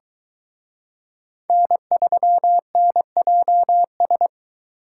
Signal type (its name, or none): Morse code